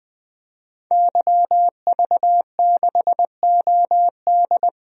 Morse code, heard as a keyed tone, about 21 wpm, 700 Hz. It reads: YV6OD
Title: DXCC-100 Call Signs - 5 Characters - 20wpm